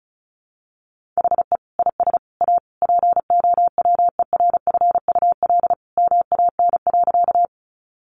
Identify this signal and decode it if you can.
Morse code, heard as a keyed tone, about 34 wpm, 700 Hz. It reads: HE IS A POWERFUL MAN.
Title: Sentences from Top 1300 Words 35wpm